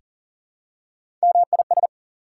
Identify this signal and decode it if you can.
Morse code, heard as a keyed tone, about 40 wpm, 700 Hz. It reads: MIS
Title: Prefixes and Suffixes 40wpm